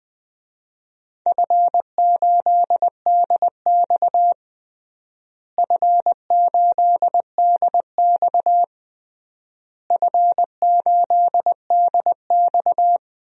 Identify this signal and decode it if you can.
Morse code, heard as a keyed tone, about 20 wpm, 700 Hz. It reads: F8DX F8DX F8DX
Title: DXCC-100 Call Signs - 4 Characters - Repeated 3x - 20wpm